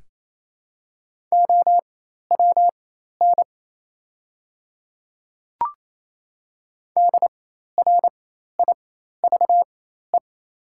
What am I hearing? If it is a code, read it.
Morse code, heard as a keyed tone, about 28 wpm, spaced as for 13 wpm, 700 Hz. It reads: OWN DRIVE
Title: 4x Character Spacing - Top 500 Words - 28wpm